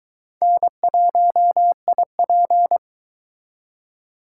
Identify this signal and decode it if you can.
Morse code, heard as a keyed tone, about 24 wpm, 700 Hz. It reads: N1IP